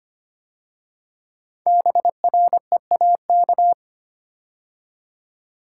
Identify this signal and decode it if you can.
Morse code, heard as a keyed tone, about 25 wpm, 700 Hz. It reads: BREAK
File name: 5 Letter Words - First 3 Letters Given - 25wpm